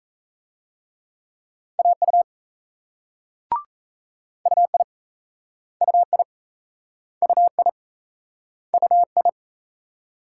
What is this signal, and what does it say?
Morse code, heard as a keyed tone, about 39 wpm, 700 Hz. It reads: AU UI UI UI UI